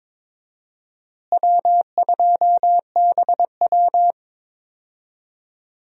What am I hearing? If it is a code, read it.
Morse code, heard as a keyed tone, about 22 wpm, 700 Hz. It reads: W2BW
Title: USA Call Signs - 4 Characters - Repeated 3x - 22wpm